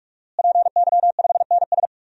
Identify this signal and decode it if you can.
Morse code, heard as a keyed tone, about 45 wpm, 700 Hz. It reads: PY5NS
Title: DXCC-100 Call Signs - 5 Characters - Repeated 3x - 45wpm